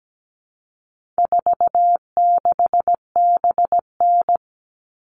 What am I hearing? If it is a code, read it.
Morse code, heard as a keyed tone, about 17 wpm, 700 Hz. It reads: 46BN
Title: All Cut Numbers - 4-Digit 17wpm